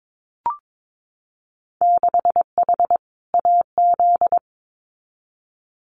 Morse code, heard as a keyed tone, about 22 wpm, 700 Hz. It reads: 6H AZ
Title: Winter Field Day Exchanges Encore 22wpm